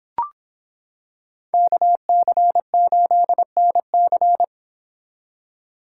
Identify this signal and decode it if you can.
Morse code, heard as a keyed tone, about 25 wpm, 700 Hz. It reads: KC8NC